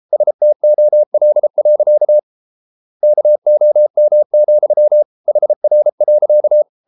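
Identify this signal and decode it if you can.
Morse code, heard as a keyed tone, about 33 wpm, 600 Hz. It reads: STOL. KOM, HR.
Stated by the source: Morse code at 33 wpm, QRQ.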